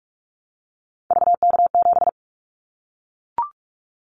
Morse code, heard as a keyed tone, about 46 wpm, 700 Hz. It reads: VK7
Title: Callsign Prefixes for Top 50 DXCC - 45wpm